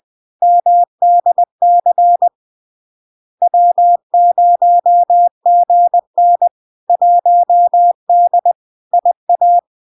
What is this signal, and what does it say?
Morse code, heard as a keyed tone, about 20 wpm, 700 Hz. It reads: MDC W0GN 1D IA
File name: LARK-FD-2022-Log-in-CW-20wpm-700-Hz